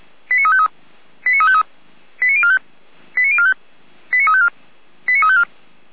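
Five-tone selective call (ZVEI), heard as quick runs of five tones, tones of about 70 ms. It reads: 89353 89353 89045 89045 89354 89354
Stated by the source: ZVEI five-tone call received by SDR.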